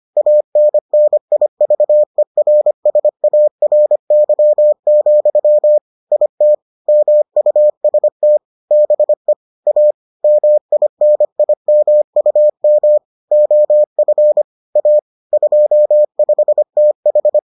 Morse code, heard as a keyed tone, about 25 wpm, 600 Hz. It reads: ANNIVERSARY, IT MUST BE A MINIMUM OF A 25TH